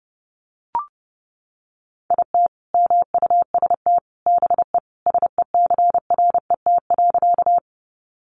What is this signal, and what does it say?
Morse code, heard as a keyed tone, about 30 wpm, 700 Hz. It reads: IT MUST BE SECRET.
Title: Sentences from Top 1200 Words 30wpm